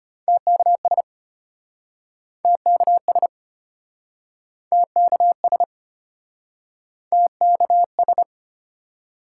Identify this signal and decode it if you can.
Morse code, heard as a keyed tone, about 33 wpm, 700 Hz. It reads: TKS TKS TKS TKS